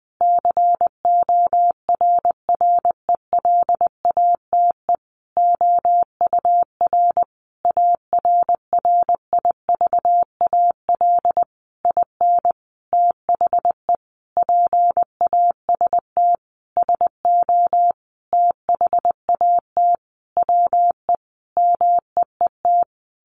Morse code, heard as a keyed tone, about 20 wpm, 700 Hz. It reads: CORRELATE OUR ARRIVAL IN THE PAST SO THAT WE MEET
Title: A Sound Of Thunder 20wpm